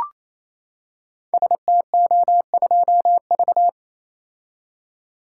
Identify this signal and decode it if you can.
Morse code, heard as a keyed tone, about 27 wpm, 700 Hz. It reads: STO2V
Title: Code Groups - 5 Letter:Numbers - 28wpm